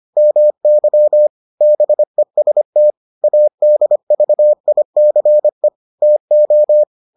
Morse code, heard as a keyed tone, about 25 wpm, 600 Hz. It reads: MY BEST ADVICE TO